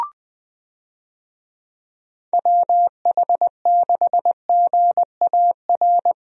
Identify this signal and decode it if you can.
Morse code, heard as a keyed tone, about 20 wpm, 700 Hz. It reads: WH6GAR
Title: USA Call Signs - 6 Characters - Repeated 3x - 20wpm